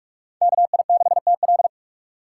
Morse code, heard as a keyed tone, about 46 wpm, 700 Hz. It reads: KI6TL
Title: USA Call Signs - 6x Character Spacing - 45wpm